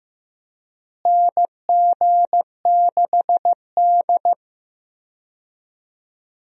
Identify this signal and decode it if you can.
Morse code, heard as a keyed tone, about 15 wpm, 700 Hz. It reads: NG6D
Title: USA Call Signs - 4 Characters - Repeated 3x - 15wpm